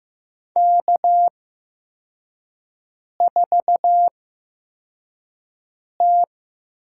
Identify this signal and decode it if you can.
Morse code, heard as a keyed tone, about 15 wpm, spaced as for 4 wpm, 700 Hz. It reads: K4T
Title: USA Call Signs - 5 Characters - 8x Character Spacing - 15wpm